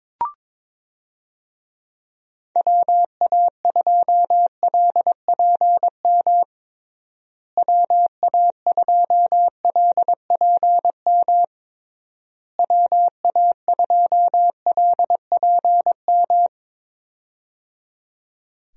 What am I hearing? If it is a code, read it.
Morse code, heard as a keyed tone, about 22 wpm, 700 Hz. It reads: WA2LPM WA2LPM WA2LPM
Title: USA Call Signs - 6 Characters - Repeated 3x - 22wpm